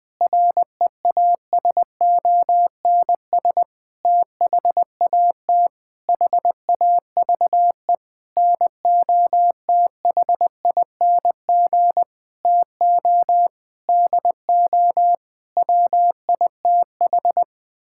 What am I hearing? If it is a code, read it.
Morse code, heard as a keyed tone, about 21 wpm, 700 Hz. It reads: REASONS THAT HAVE NOTHING TO DO WITH
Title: Sentences with 8-Word Phrases - Spoken Beginning - 20wpm